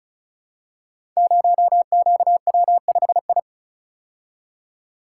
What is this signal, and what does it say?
Morse code, heard as a keyed tone, about 36 wpm, 700 Hz. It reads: 0QW5I
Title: Code Groups - 5 Letter:Numbers - 35wpm